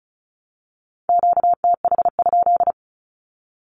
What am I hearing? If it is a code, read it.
Morse code, heard as a keyed tone, about 34 wpm, 700 Hz. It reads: QTH?
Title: Q-Codes 35wpm